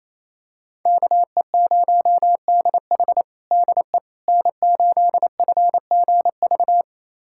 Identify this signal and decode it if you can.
Morse code, heard as a keyed tone, about 28 wpm, 700 Hz. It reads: KE0DH DE N8FGV
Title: USA Callsign DE Callsign - 5 Characters - 28wpm